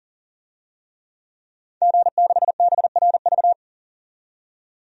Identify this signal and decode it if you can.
Morse code, heard as a keyed tone, about 39 wpm, 700 Hz. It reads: G6BRV
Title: DXCC-100 Call Signs - 5 Characters - 40wpm